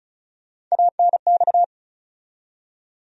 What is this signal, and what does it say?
Morse code, heard as a keyed tone, about 36 wpm, 700 Hz. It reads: ANX
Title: USA Call Sign Postfixes - 3 Letters 35wpm